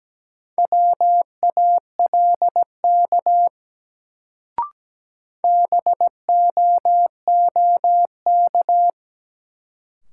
Morse code, heard as a keyed tone, about 17 wpm, 700 Hz. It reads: WALK BOOK
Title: Sentences from Top 300 Words - New Word Drill - 17wpm